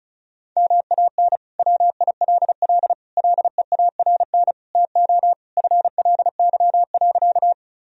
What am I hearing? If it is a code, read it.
Morse code, heard as a keyed tone, about 34 wpm, 700 Hz. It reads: MAN WILL LEARN TO FLY.
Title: Sentences from Top 1000 Words - Review Part 2 - 35wpm